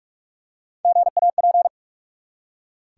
Morse code, heard as a keyed tone, about 46 wpm, 700 Hz. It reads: GAP
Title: Three Letter Words - 45wpm